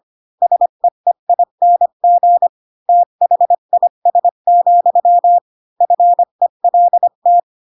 Morse code, heard as a keyed tone, about 25 wpm, 700 Hz. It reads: SEEING THIS, FELT
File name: Pinocchio / Pinocchio ch3 20 25 ik1hge